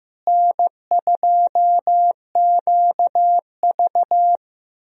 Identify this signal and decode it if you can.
Morse code, heard as a keyed tone, about 15 wpm, 700 Hz. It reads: N2QV
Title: USA Call Signs - 4 Characters - Repeated 2x - 15wpm